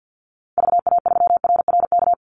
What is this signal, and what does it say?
Morse code, heard as a keyed tone, about 48 wpm, 700 Hz. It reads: VA3RRK
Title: Call Signs Repeated 3x - 50wpm